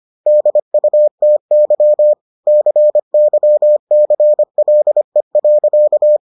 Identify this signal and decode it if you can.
Morse code, heard as a keyed tone, about 25 wpm, 600 Hz. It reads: DUTY CYCLE.